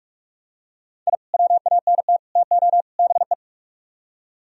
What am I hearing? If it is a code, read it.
Morse code, heard as a keyed tone, about 44 wpm, 700 Hz. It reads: I WANT TO BE